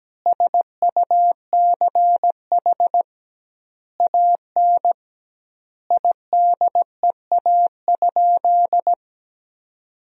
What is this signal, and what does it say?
Morse code, heard as a keyed tone, about 17 wpm, 700 Hz. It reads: SUCH AN IDEA?
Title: 2x Word-Spacing - Sentences from Top 400 Words - 17wpm